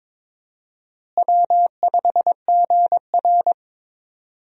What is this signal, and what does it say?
Morse code, heard as a keyed tone, about 21 wpm, 700 Hz. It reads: W5GR
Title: USA Call Signs - 4 Characters - Repeated 3x - 22wpm